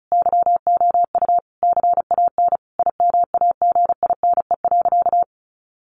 Morse code, heard as a keyed tone, about 34 wpm, 700 Hz. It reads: YOU CAN IMAGINE.
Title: Sentences from Top 1400 Words 35wpm